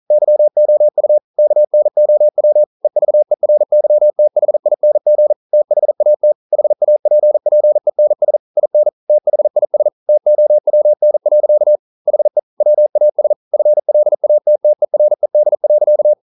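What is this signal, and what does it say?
Morse code, heard as a keyed tone, about 42 wpm, 600 Hz. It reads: YOU KNOW EVERYTHING THAT HAPPENS IN THIS TOWN. HE WAS FLATTERED.
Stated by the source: Morse code at 40 wpm, practice text, CW med Gnister.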